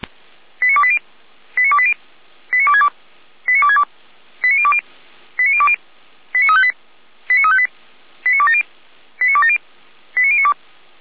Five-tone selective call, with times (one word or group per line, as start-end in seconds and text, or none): 0.6-1.0 89280
1.6-1.9 89280
2.5-2.9 89262
3.5-3.8 89262
4.4-4.8 89020
5.4-5.8 89020
6.3-6.7 89358
7.3-7.7 89358
8.3-8.6 89270
9.2-9.6 89270
10.2-10.5 89092